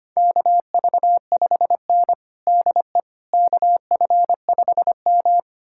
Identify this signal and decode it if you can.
Morse code, heard as a keyed tone, about 25 wpm, 700 Hz. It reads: KV5N DE KF5M
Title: USA Callsign DE Callsign - 4 Characters - 25wpm